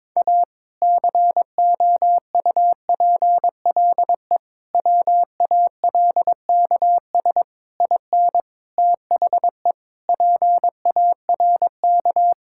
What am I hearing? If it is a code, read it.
Morse code, heard as a keyed tone, about 22 wpm, 700 Hz. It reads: A COUPLE WALKS IN THE PARK
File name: practice / 22wpm / Sentences with 2-Word Phrases - 22wpm